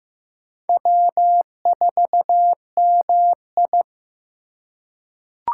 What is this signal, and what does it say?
Morse code, heard as a keyed tone, about 15 wpm, 700 Hz. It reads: W4MI
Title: USA Call Signs - 4 Characters - 2x Character Spacing - 15wpm